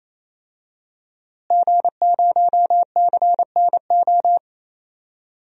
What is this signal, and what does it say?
Morse code, heard as a keyed tone, about 28 wpm, 700 Hz. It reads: G0CNO
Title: DXCC-100 Call Signs - 5 Characters - Repeated 3x - 28wpm